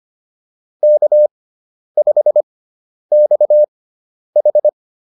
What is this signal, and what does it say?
Morse code, heard as a keyed tone, about 25 wpm, spaced as for 10 wpm, 600 Hz. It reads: K5XH